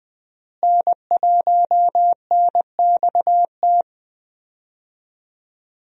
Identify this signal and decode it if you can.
Morse code, heard as a keyed tone, about 20 wpm, 700 Hz. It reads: N1NXT
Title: USA Call Signs - 5 Characters - Repeated 2x - 20wpm